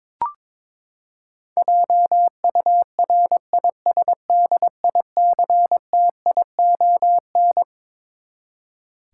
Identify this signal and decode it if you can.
Morse code, heard as a keyed tone, about 21 wpm, 700 Hz. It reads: JURISDICTION
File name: Twelve Letter Words - 22wpm